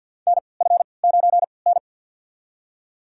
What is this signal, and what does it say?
Morse code, heard as a keyed tone, about 53 wpm, spaced as for 29 wpm, 700 Hz. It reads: NF9N
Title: USA Call Signs - 4 Characters - 3x Character Spacing - 50wpm